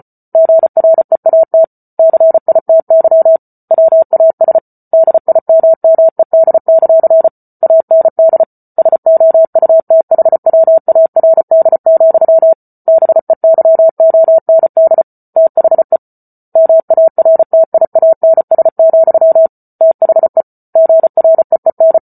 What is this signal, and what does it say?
Morse code, heard as a keyed tone, about 34 wpm, 650 Hz. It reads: GREAT CITY WAS DIMMED; AND SOUTHWARD, BEYOND THE MARTIANS, THE GREEN